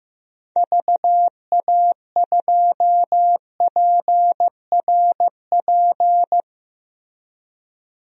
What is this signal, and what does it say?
Morse code, heard as a keyed tone, about 15 wpm, 700 Hz. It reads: VA2PRP
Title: DXCC-100 Call Signs - 6 Characters - Repeated 2x - 15wpm